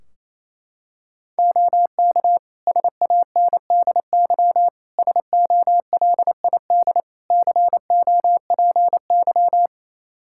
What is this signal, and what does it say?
Morse code, heard as a keyed tone, about 28 wpm, 700 Hz. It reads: OK SANDY SOLID COPY
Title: K1USN SST QSO Practice 28wpm